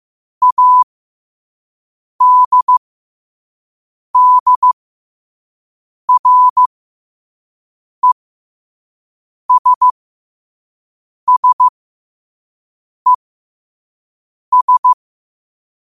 Morse code, heard as a keyed tone, about 15 wpm, spaced as for 5 wpm, 1000 Hz. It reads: ADDRESSES
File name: qotdfiles / 5wpm